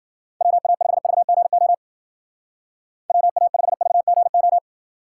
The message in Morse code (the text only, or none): WA5VGO WA5VGO